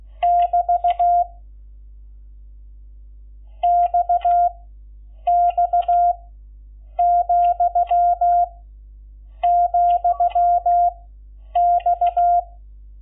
Morse code, heard as a keyed tone, about 16 wpm, 700 Hz. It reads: <BT> X X , , X
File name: Lecciones / CLASE Nº11